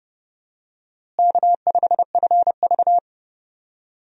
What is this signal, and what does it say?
Morse code, heard as a keyed tone, about 30 wpm, 700 Hz. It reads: K5FV
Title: USA Call Signs - 4 Characters - Repeated 3x - 30wpm